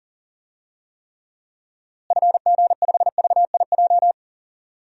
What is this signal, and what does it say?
Morse code, heard as a keyed tone, about 40 wpm, 700 Hz. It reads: FG5VIJ
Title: DXCC-100 Call Signs - 6 Characters - Repeated 2x - 40wpm